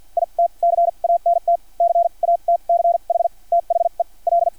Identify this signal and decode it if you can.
Morse code, heard as a keyed tone, about 50 wpm, spaced as for 39 wpm, 680 Hz. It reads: IT CANT CATCH THE L
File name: archived / CRSnet